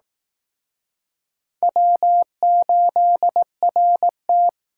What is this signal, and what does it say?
Morse code, heard as a keyed tone, about 19 wpm, 700 Hz. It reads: W8RT